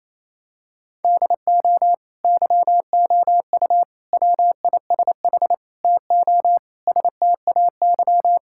Morse code, heard as a keyed tone, about 28 wpm, 700 Hz. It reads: DO YOU WISH TO STAY
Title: Sentences from Top 1600 Words 28wpm